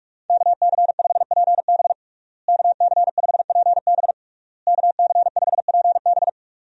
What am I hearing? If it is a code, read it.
Morse code, heard as a keyed tone, about 46 wpm, 700 Hz. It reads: KC5PB KC5PB KC5PB
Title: USA Call Signs - 5 Characters - Repeated 3x - 45wpm